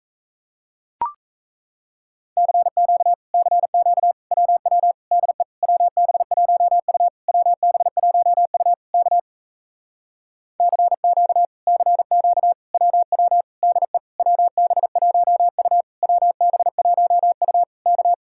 Morse code, heard as a keyed tone, about 38 wpm, 700 Hz. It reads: CQ CQ WW DE WB1U WB1U K CQ CQ WW DE WB1U WB1U K